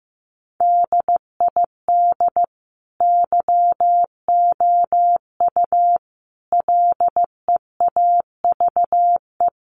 Morse code, heard as a keyed tone, about 15 wpm, 700 Hz. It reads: DID YOU LEAVE
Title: Sentences from Top 1000 Words 15wpm